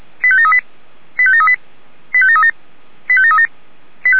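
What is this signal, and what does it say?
Five-tone selective call (ZVEI), heard as quick runs of five tones, tones of about 70 ms. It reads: 85738 85738 85737 85737
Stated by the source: Five-tone ZVEI selective call 85725 received on an SDR.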